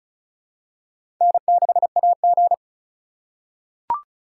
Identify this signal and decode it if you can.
Morse code, heard as a keyed tone, about 36 wpm, 700 Hz. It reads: N6AG
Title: USA Call Signs - 4x Character Spacing - 35wpm